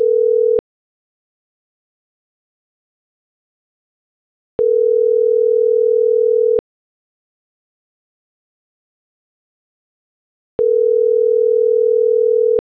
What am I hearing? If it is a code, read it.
Ringback tone (North America), 440 + 480 Hz.